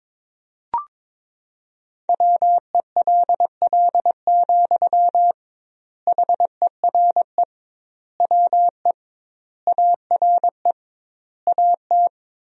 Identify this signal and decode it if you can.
Morse code, heard as a keyed tone, about 22 wpm, 700 Hz. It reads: WELL, HERE WE ARE AT
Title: 2x Word-Spacing - Sentences from Top 300 Words - 22wpm